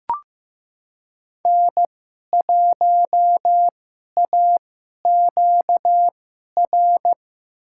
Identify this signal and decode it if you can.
Morse code, heard as a keyed tone, about 15 wpm, spaced as for 11 wpm, 700 Hz. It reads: N1AQR
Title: USA Call Signs - 5 Characters - 2x Character Spacing - 15wpm